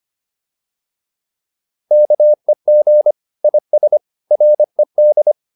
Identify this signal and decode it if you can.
Morse code, heard as a keyed tone, about 25 wpm, 600 Hz. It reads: KEG IS RED